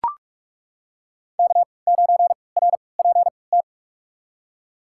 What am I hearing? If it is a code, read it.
Morse code, heard as a keyed tone, about 41 wpm, spaced as for 25 wpm, 700 Hz. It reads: K9RPT
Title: USA Call Signs - 5 Characters - 3x Character Spacing - 45wpm